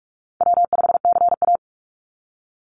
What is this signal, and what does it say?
Morse code, heard as a keyed tone, about 46 wpm, 700 Hz. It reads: W5CA
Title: USA Call Signs 45wpm